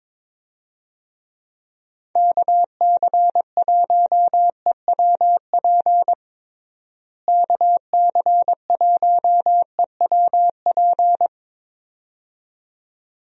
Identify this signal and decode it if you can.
Morse code, heard as a keyed tone, about 22 wpm, 700 Hz. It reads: KC1EWP KC1EWP
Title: USA Call Signs - 6 Characters - Repeated 2x - 22wpm